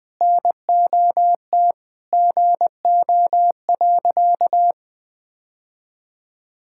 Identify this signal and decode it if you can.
Morse code, heard as a keyed tone, about 20 wpm, 700 Hz. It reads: NOT GO.
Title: Sentences with 5-Word Phrases - 20wpm